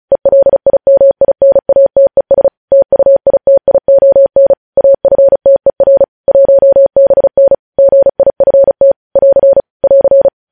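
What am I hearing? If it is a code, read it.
Morse code, heard as a keyed tone, about 36 wpm, 550 Hz. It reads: ELIMINATES TUITION AFTER 1BN GIFT <AR> <AR>
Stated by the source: Morse code practice reading news headlines at 35 words per minute.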